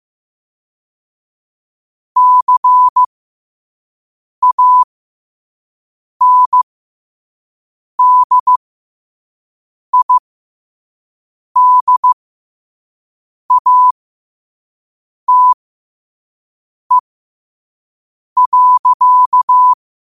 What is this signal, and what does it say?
Morse code, heard as a keyed tone, about 15 wpm, spaced as for 5 wpm, 1000 Hz. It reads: CANDIDATE.
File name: qotdfiles / 5wpm